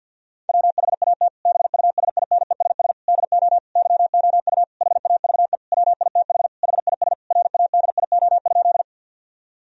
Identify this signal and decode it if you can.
Morse code, heard as a keyed tone, about 50 wpm, 700 Hz. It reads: WHAT BUSINESS DO YOU HAVE WITH HIS RADIO?
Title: Sentences from Top 1000 Words - Review Part 3 - 50wpm